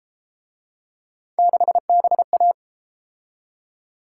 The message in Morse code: 6BA

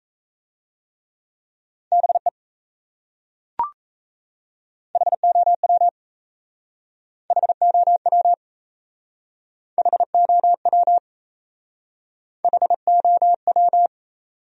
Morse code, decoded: BE HOW HOW HOW HOW

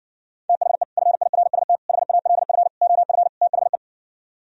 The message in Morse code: THE FIRST HALF OF THE

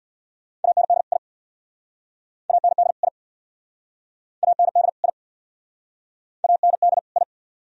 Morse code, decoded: AND I AND I AND I AND I

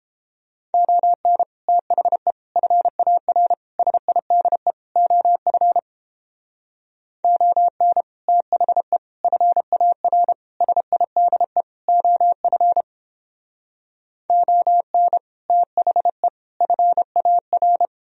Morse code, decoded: ON THE FAR SIDE OF ON THE FAR SIDE OF ON THE FAR

